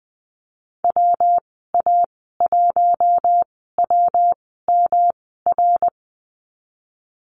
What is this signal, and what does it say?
Morse code, heard as a keyed tone, about 20 wpm, spaced as for 15 wpm, 700 Hz. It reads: WA1WMR
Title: 122 Lesson 20g - Number 3 - Call Signs 20wpm